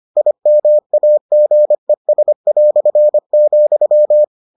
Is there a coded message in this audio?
Yes: Morse code from 0.2 to 4.2 s, heard as a keyed tone, about 25 wpm, 600 Hz. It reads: IMAGES",